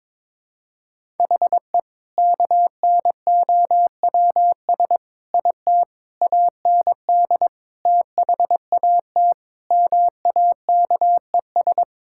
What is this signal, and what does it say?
Morse code, heard as a keyed tone, about 22 wpm, 700 Hz. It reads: HE KNOWS IT AND THAT MAKES
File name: practice / 22wpm / Sentences with 2-Word Phrases - 22wpm